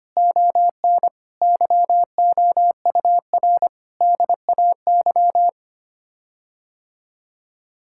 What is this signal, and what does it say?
Morse code, heard as a keyed tone, about 25 wpm, 700 Hz. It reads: ON YOUR DAY
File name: Sets of 9 Words - 25wpm